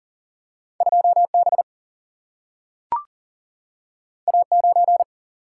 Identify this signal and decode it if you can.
Morse code, heard as a keyed tone, about 41 wpm, 700 Hz. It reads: 2B A9
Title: Single-Letter Single-Number 40wpm